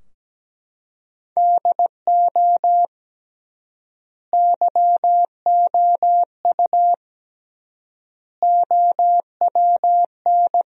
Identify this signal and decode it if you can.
Morse code, heard as a keyed tone, about 17 wpm, 700 Hz. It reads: DO YOU OWN
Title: 3x Word-Spacing - Sentences from Top 400 Words - 17wpm